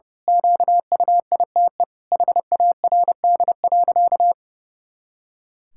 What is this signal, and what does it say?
Morse code, heard as a keyed tone, about 30 wpm, 700 Hz. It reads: QUITE HARD.
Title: Pangram Challenge v2 - 30wpm